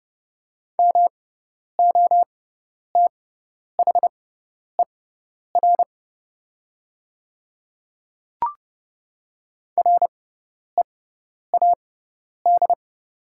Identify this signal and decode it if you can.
Morse code, heard as a keyed tone, about 30 wpm, spaced as for 10 wpm, 690 Hz. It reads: MOTHER READ
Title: 6x Character Spacing - Top 200 Words - 30wpm